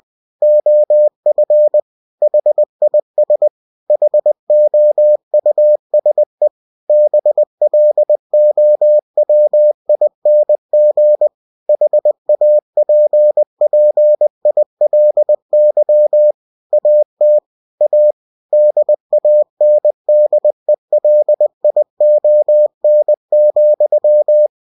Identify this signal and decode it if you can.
Morse code, heard as a keyed tone, about 20 wpm, 600 Hz. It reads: OF HIS HOUSE BLOWING HAPPILY AT A DANDELION,
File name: cw-books / winnie-the-pooh / winnie-20wpm-page-0078